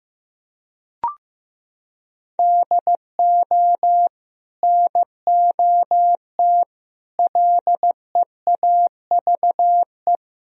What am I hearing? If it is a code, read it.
Morse code, heard as a keyed tone, about 16 wpm, 700 Hz. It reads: DO NOT LEAVE
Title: Sentences from Top 1000 Words - Review Part 3 - 15wpm